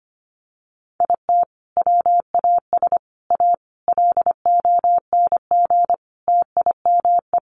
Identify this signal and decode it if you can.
Morse code, heard as a keyed tone, about 25 wpm, 700 Hz. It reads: IT WAS A LONG TIME
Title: Sentences from Top 500 Words 25wpm